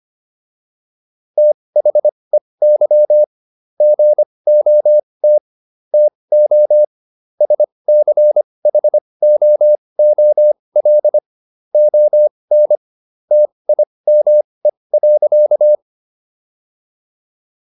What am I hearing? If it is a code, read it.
Morse code, heard as a keyed tone, about 25 wpm, spaced as for 20 wpm, 600 Hz. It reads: THEY GOT TO SCHOOL ON TIME.